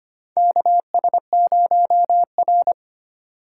Morse code, heard as a keyed tone, about 25 wpm, 700 Hz. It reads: KS0R
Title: USA Call Signs - 2x Character Spacing - 25wpm